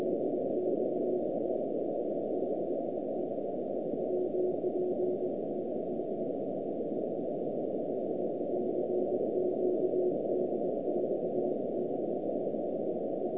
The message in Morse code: none